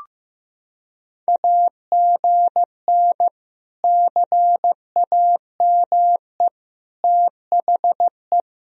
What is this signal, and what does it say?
Morse code, heard as a keyed tone, about 15 wpm, 700 Hz. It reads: AGN CAME THE
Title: Sentences from Top 500 Words and Abbreviations 15wpm